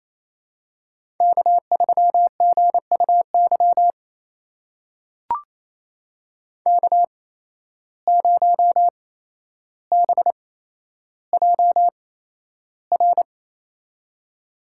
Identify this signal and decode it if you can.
Morse code, heard as a keyed tone, about 28 wpm, spaced as for 12 wpm, 700 Hz. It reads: K3GUY K0BJR
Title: USA Call Signs - 5 Characters - 8x Character Spacing - 28wpm